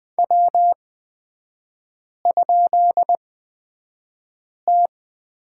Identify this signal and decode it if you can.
Morse code, heard as a keyed tone, about 20 wpm, spaced as for 5 wpm, 700 Hz. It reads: W?T
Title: 224 Lesson 36e - BT - Three Characters 20wpm